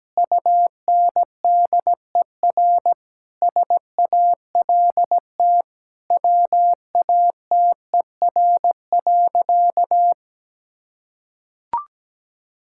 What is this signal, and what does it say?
Morse code, heard as a keyed tone, about 17 wpm, 700 Hz. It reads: UNDER SALT WATER.
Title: Sentences from Top 1000 Words - Review Part 5 - 17wpm